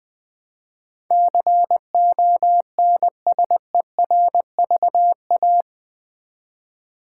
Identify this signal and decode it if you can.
Morse code, heard as a keyed tone, about 20 wpm, 700 Hz. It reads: CONSERVA
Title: Build A Word - 4 Letters At A Time - 12-Letter Words - 20wpm